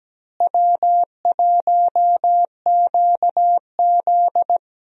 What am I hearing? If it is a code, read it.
Morse code, heard as a keyed tone, about 17 wpm, 700 Hz. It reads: W1QZ
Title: USA Call Signs - 4 Characters - Repeated 2x - 17wpm